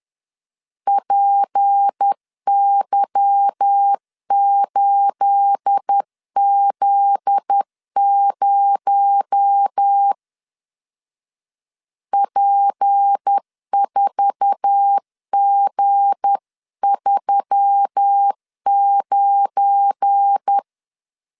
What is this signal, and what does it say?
Morse code, heard as a keyed tone, about 10 wpm, 800 Hz. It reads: PY8Z0 P4G39